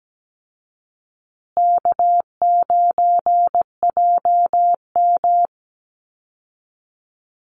Morse code, heard as a keyed tone, about 17 wpm, 700 Hz. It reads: K9JM
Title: Easy Call Signs Repeated 2x - 17wpm